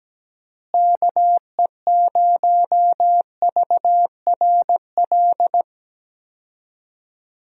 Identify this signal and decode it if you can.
Morse code, heard as a keyed tone, about 17 wpm, 700 Hz. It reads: KE0VRL
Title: USA Call Signs - 6 Characters - Repeated 3x - 17wpm